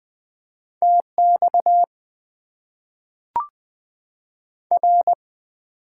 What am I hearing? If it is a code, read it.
Morse code, heard as a keyed tone, about 20 wpm, 700 Hz. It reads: TX R
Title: Top 90 QSO Elements - 8x Character Spacing - 20wpm